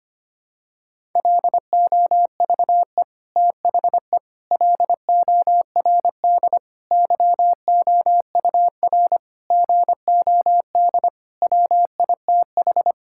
Morse code, heard as a keyed tone, about 25 wpm, 700 Hz. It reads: LOVE THE LORD YOUR GOD WITH
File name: Sets of 9 Words - 25wpm